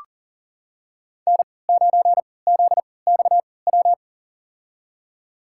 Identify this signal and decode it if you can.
Morse code, heard as a keyed tone, about 40 wpm, spaced as for 23 wpm, 700 Hz. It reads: N9ZXW
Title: USA Call Signs - 5 Characters - 3x Character Spacing - 40wpm